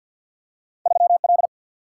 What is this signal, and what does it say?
Morse code, heard as a keyed tone, about 51 wpm, 700 Hz. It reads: 3L
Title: Single-Letter Single-Number 50wpm